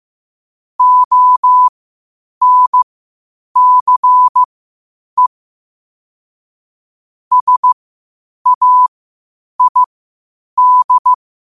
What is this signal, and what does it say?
Morse code, heard as a keyed tone, about 15 wpm, spaced as for 8 wpm, 1000 Hz. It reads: ONCE SAID
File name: qotdfiles / 8wpm